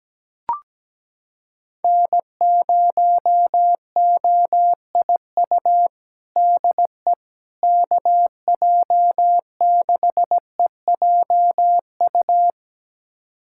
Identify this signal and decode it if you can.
Morse code, heard as a keyed tone, about 17 wpm, 700 Hz. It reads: N0OIU DE KJ6EJU
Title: USA Callsign DE Callsign 17wpm